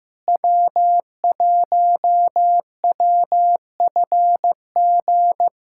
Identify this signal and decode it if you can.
Morse code, heard as a keyed tone, about 15 wpm, 700 Hz. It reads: W1WFG
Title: USA Callsign DE Callsign - 5 Characters - 15wpm